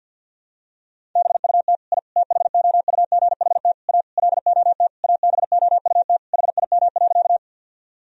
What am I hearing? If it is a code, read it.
Morse code, heard as a keyed tone, about 51 wpm, 700 Hz. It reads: BUT I THOUGHT A LOT ABOUT HIM.
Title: Sentences from Top 1000 Words - Review Part 3 - 50wpm